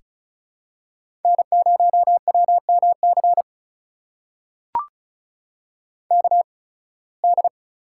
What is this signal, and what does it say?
Morse code, heard as a keyed tone, about 36 wpm, spaced as for 23 wpm, 700 Hz. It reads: N0WMC KD
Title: USA Call Signs - 8x Character Spacing - 35wpm